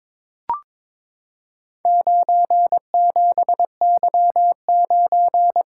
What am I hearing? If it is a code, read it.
Morse code, heard as a keyed tone, about 22 wpm, 700 Hz. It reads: 97Y9
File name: practice / 22wpm / Four Letter-Number Combinations - 22wpm